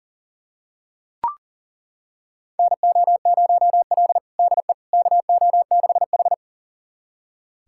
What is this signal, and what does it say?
Morse code, heard as a keyed tone, about 40 wpm, 700 Hz. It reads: NO0L DE KO6H
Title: USA Callsign DE Callsign - 4 Characters - 40wpm